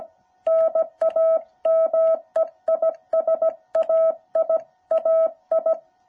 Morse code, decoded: NAME IS AI AI

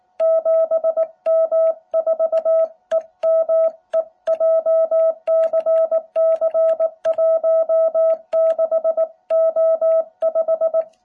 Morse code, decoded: 7M4EME JCC16O5

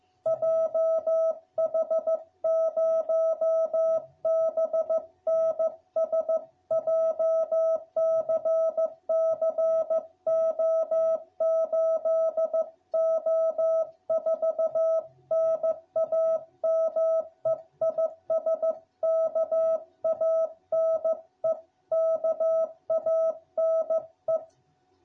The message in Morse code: JH0BNS JCC O8O4 NAME IS KANE KANE

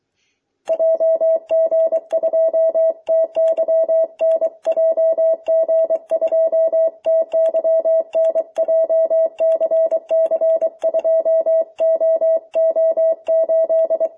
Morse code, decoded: JG2TYN JG2TYN JCC2OO8

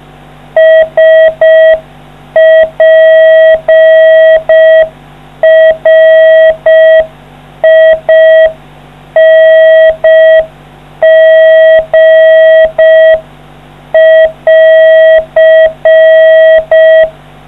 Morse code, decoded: SPRING<AR>

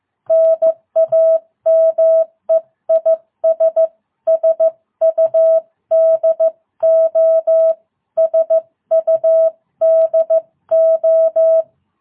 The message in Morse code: NAME IS SUDO SUDO

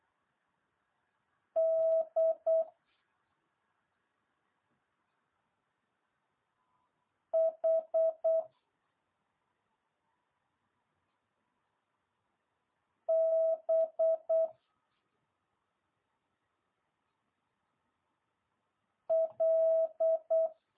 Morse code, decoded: D H B L

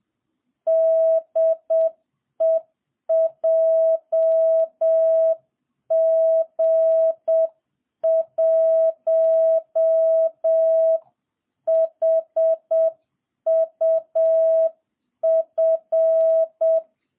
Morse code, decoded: DE JG1HUF

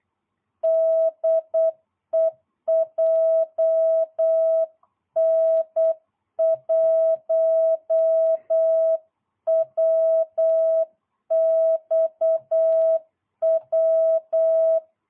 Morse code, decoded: DE JN1WXW

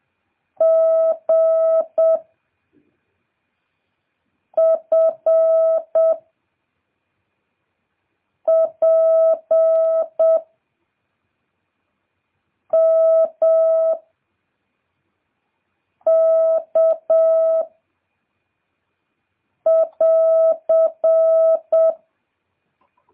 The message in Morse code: G F P M K <AR>

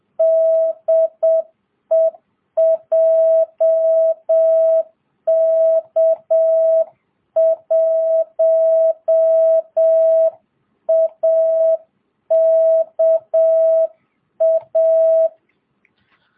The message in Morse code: DE JK1AKA

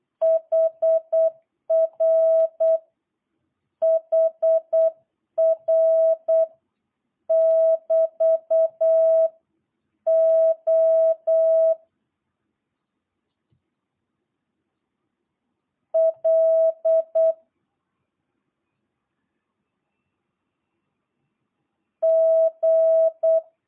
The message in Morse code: HR HR <BT> O L G

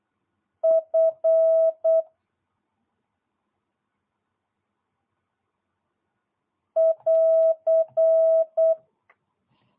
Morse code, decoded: F <AR>

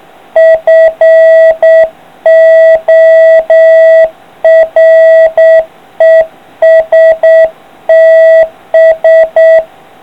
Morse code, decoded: FORESTS

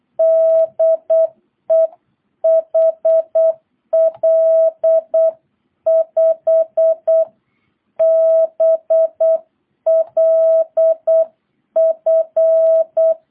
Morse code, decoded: DE HL5BLF

